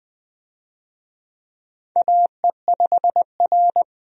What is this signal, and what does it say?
Morse code, heard as a keyed tone, about 20 wpm, 700 Hz. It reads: AE5R
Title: USA Call Signs - 4 Characters - 20wpm